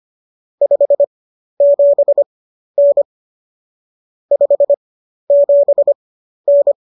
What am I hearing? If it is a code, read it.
Morse code, heard as a keyed tone, about 25 wpm, spaced as for 12 wpm, 570 Hz. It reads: 57N 57N
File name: qso008 12